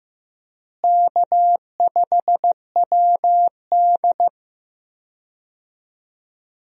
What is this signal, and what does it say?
Morse code, heard as a keyed tone, about 15 wpm, 700 Hz. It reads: K5WD
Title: USA Call Signs - 4 Characters - Repeated 3x - 15wpm